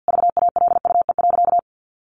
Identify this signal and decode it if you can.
Morse code, heard as a keyed tone, about 50 wpm, 700 Hz. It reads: VALUE.
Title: Sentences from Top 1400 Words 50wpm